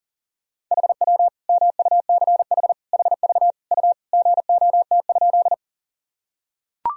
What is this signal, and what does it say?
Morse code, heard as a keyed tone, about 40 wpm, 700 Hz. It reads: HW MUCH HV U GOT?
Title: Sentences from Top 300 Words and Abbreviations 40wpm